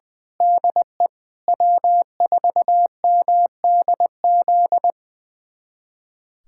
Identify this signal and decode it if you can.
Morse code, heard as a keyed tone, about 20 wpm, 700 Hz. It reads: DE W4MDZ
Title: USA Callsign DE Callsign - 5 Characters - 20wpm